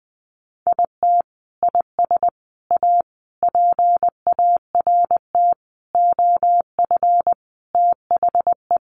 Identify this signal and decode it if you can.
Morse code, heard as a keyed tone, about 20 wpm, 700 Hz. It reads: IT IS A PART OF THE
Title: Sentences from Top 1400 Words 20wpm